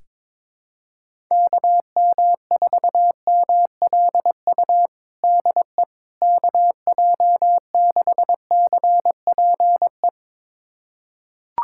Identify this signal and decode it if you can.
Morse code, heard as a keyed tone, about 22 wpm, 700 Hz. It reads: KM4MLU DE KJ6CPE
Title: USA Callsign DE Callsign 22wpm